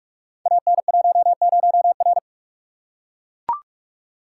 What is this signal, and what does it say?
Morse code, heard as a keyed tone, about 44 wpm, 700 Hz. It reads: AN10R
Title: Code Groups - 5 Letter:Numbers - 45wpm